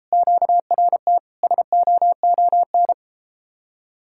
Morse code, heard as a keyed tone, about 33 wpm, 700 Hz. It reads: QRT SOON